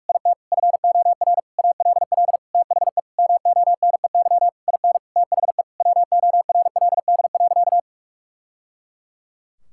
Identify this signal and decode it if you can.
Morse code, heard as a keyed tone, about 46 wpm, 700 Hz. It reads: IT FOR ALL THE MONEY IN THE WORLD.
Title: Sentences from Top 1000 Words - Review Part 6 - 45wpm